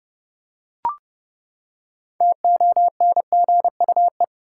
Morse code, heard as a keyed tone, about 30 wpm, 700 Hz. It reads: TONGUE
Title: Six Letter Words - 30wpm